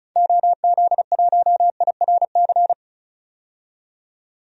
Morse code, heard as a keyed tone, about 34 wpm, 700 Hz. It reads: OZ1IRC
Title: DXCC-100 Call Signs - 6 Characters - 35wpm